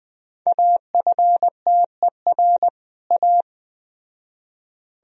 Morse code, heard as a keyed tone, about 20 wpm, 700 Hz. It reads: AFTER A
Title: Speed-Racing - Sets of 2 Words 17wpm